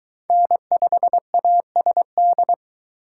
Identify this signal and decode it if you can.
Morse code, heard as a keyed tone, about 22 wpm, 700 Hz. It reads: N5ASD